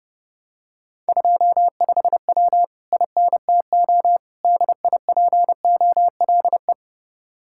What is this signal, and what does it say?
Morse code, heard as a keyed tone, about 28 wpm, 700 Hz. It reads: 25W INTO DIPOLE